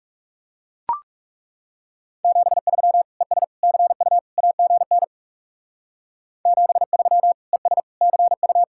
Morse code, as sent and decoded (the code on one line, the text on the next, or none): --... ...-- / . ... / -.-. ..- / .- --. -. / --... ...-- / . ... / -.-. ..-
73 ES CU AGN 73 ES CU